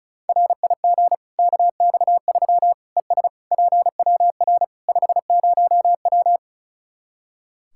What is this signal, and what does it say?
Morse code, heard as a keyed tone, about 36 wpm, 700 Hz. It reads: RIG KX3 ES PWR 50W
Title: Practice QSO - Exchange 2 - Info - 35wpm